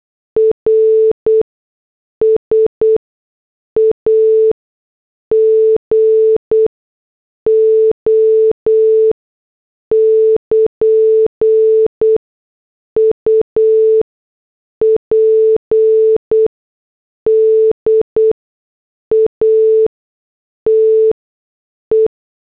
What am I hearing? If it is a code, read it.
Morse code, heard as a keyed tone, about 8 wpm, spaced as for 6 wpm, 440 Hz. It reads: RS AGO <KN> UPDATE